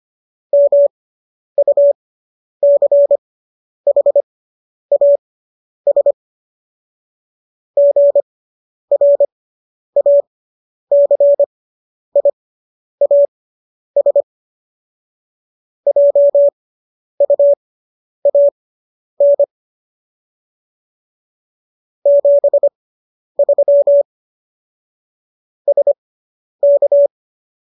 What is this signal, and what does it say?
Morse code, heard as a keyed tone, about 25 wpm, spaced as for 10 wpm, 570 Hz. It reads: MUCHAS GRACIAS JUAN 73 SK